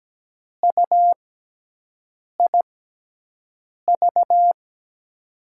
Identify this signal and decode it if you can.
Morse code, heard as a keyed tone, about 17 wpm, spaced as for 6 wpm, 700 Hz. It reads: UIV
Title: USA Call Signs - 6x Character Spacing - 17wpm